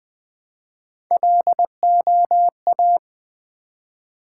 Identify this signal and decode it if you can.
Morse code, heard as a keyed tone, about 20 wpm, 700 Hz. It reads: LOA
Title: Build A Word - 1 Letter At A Time - 4-Letter Words - 20wpm